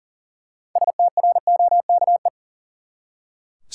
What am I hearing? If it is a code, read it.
Morse code, heard as a keyed tone, about 40 wpm, 700 Hz. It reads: STROKE
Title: Six Letter Words - 40wpm